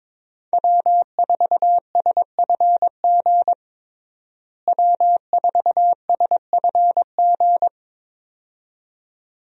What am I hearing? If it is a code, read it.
Morse code, heard as a keyed tone, about 22 wpm, 700 Hz. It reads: W4SFG W4SFG
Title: USA Call Signs - 5 Characters - Repeated 2x - 22wpm